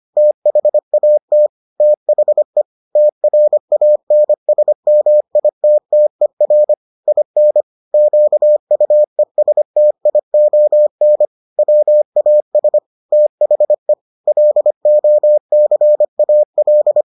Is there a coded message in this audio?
Yes: Morse code from 0.2 to 17.0 s, heard as a keyed tone, about 25 wpm, 600 Hz. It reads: THAT THE TRANSMITTER IN QUESTION WAS THE LOCAL